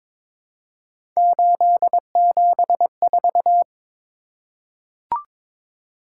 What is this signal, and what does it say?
Morse code, heard as a keyed tone, about 22 wpm, 700 Hz. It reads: 874
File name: practice / 22wpm / Three Letter-Number Combinations - 22wpm